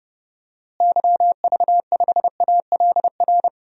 Morse code, heard as a keyed tone, about 30 wpm, 700 Hz. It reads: YV5ALR
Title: DXCC-100 Call Signs - 6 Characters - Repeated 2x - 30wpm